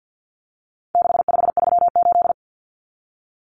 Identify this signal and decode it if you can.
Morse code, heard as a keyed tone, about 50 wpm, 700 Hz. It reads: 6538
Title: Common Cut Numbers - 4-Digit 50wpm